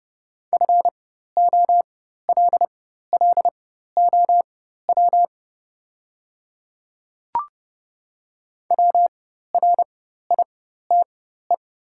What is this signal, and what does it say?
Morse code, heard as a keyed tone, about 30 wpm, spaced as for 14 wpm, 700 Hz. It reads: FOLLOW WRITE
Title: 4x Character Spacing - Top 200 Words - 30wpm